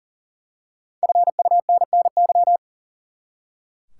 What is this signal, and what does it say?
Morse code, heard as a keyed tone, about 40 wpm, 700 Hz. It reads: FUNNY
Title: Five Letter Words - 40wpm